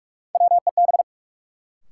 Morse code, heard as a keyed tone, about 46 wpm, 700 Hz. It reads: WEB
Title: Sentences from Top 1000 Words - Review Part 4 - New Word Drill - 45wpm